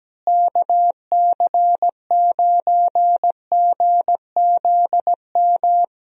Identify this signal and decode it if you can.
Morse code, heard as a keyed tone, about 17 wpm, 700 Hz. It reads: KC9GZM
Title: USA Call Signs - 6 Characters - Repeated 3x - 17wpm